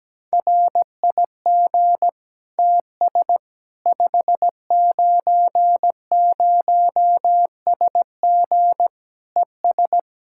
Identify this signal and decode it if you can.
Morse code, heard as a keyed tone, about 17 wpm, 700 Hz. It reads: RIG TS 590SG ES
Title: Practice QSO - Exchange 2 - Info - 17wpm